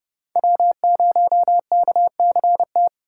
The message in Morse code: W0KCT

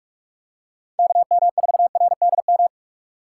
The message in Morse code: KM4RDM

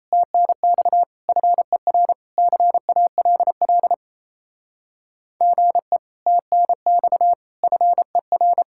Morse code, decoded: TNX FER CALL GE TNX FER

